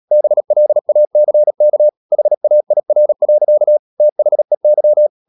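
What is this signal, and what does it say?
Morse code, heard as a keyed tone, about 37 wpm, 600 Hz. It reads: BLACK HAIR. THEY
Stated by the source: Morse code practice at 35–39 wpm sending English text of Ken Follett's novel Never, from CW med Gnister.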